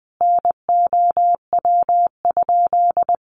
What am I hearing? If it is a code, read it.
Morse code, heard as a keyed tone, about 20 wpm, 700 Hz. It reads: NOW?
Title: Sentences from Top 200 Words 20wpm